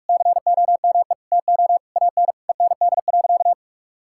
Morse code, heard as a keyed tone, about 46 wpm, 700 Hz. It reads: COME TO AN END.
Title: Sentences from Top 1000 Words - Review Part 6 - 45wpm